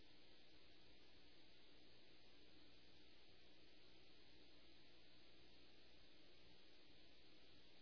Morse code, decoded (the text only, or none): none